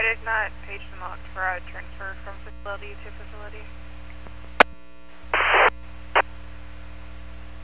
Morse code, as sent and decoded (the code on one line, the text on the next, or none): none